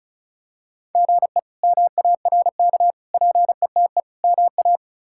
Morse code, heard as a keyed tone, about 34 wpm, 700 Hz. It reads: GE MARK PETE MA
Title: K1USN SST QSO Practice - 35wpm